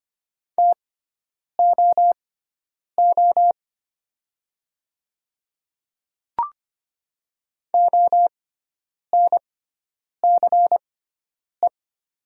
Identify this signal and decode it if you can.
Morse code, heard as a keyed tone, about 25 wpm, spaced as for 9 wpm, 700 Hz. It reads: TOO ONCE
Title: 6x Character Spacing - Top 300 Words - 25wpm